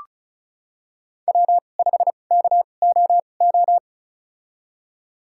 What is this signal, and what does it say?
Morse code, heard as a keyed tone, about 36 wpm, spaced as for 25 wpm, 700 Hz. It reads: W5KOO
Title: USA Call Signs - 5 Characters - 2x Character Spacing - 35wpm